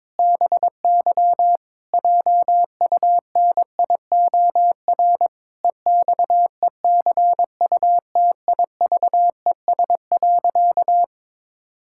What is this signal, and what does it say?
Morse code, heard as a keyed tone, about 22 wpm, 700 Hz. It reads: BY JUNIOR EXECUTIVES.
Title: Pangram Challenge v2 - 22wpm